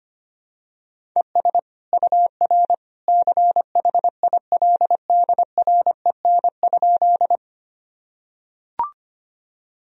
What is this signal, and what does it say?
Morse code, heard as a keyed tone, about 25 wpm, 700 Hz. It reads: ES UR CHILDREN?
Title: Sentences from Top 300 Words and Abbreviations 25wpm